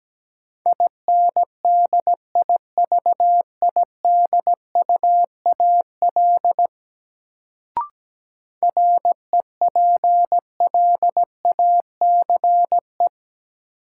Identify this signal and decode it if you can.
Morse code, heard as a keyed tone, about 17 wpm, 700 Hz. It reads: INDIVIDUAL REPLACE
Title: Sentences from Top 1400 Words - New Word Drill Alt - 17wpm